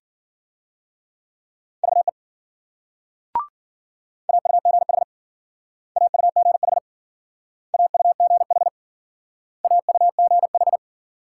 Morse code, heard as a keyed tone, about 50 wpm, 700 Hz. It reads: VE AUGH AUGH AUGH AUGH